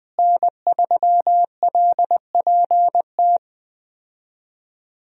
Morse code, heard as a keyed tone, about 20 wpm, 700 Hz. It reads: N3LPT
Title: USA Call Signs - 5 Characters - Repeated 2x - 20wpm